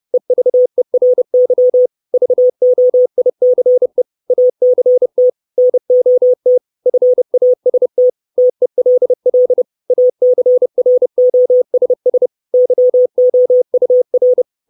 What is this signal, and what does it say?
Morse code, heard as a keyed tone, about 30 wpm, 500 Hz. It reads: EVERY VOICE ACT NOT FAST TELL ACROSS YOUR
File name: 632 Words 30WPM